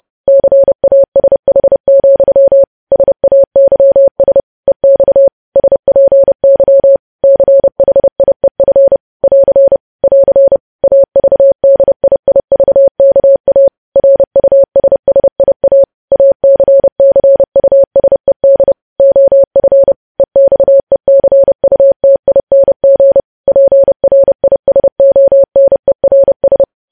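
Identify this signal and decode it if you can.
Morse code, heard as a keyed tone, about 30 wpm, 550 Hz. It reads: CASH, SAYS EX SPY CHIEF <AR> <AR> AVDIIVKA RUSSIA ACCUSED OF EXECUTING PRISONERS
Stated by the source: News headlines sent as Morse practice at 30.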